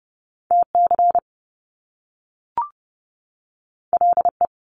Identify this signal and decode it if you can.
Morse code, heard as a keyed tone, about 30 wpm, 700 Hz. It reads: TC LE
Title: Top Two Letter Combinations in Words 30wpm